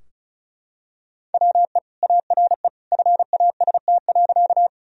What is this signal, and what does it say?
Morse code, heard as a keyed tone, about 36 wpm, 700 Hz. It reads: WE ARE FAST.
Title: Sentences from Top 1000 Words - Review Part 2 - 35wpm